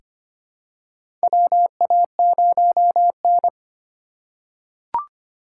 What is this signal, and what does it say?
Morse code, heard as a keyed tone, about 25 wpm, 700 Hz. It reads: WA0N
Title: USA Call Signs - 3x Character Spacing - 25wpm